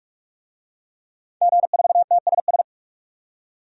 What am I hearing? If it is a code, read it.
Morse code, heard as a keyed tone, about 44 wpm, 700 Hz. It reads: G4TSS
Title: DXCC-100 Call Signs - 5 Characters - 45wpm